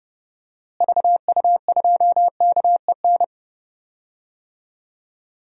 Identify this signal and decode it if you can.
Morse code, heard as a keyed tone, about 32 wpm, 700 Hz. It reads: VU2KEN
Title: DXCC-100 Call Signs - 6 Characters - Repeated 3x - 30wpm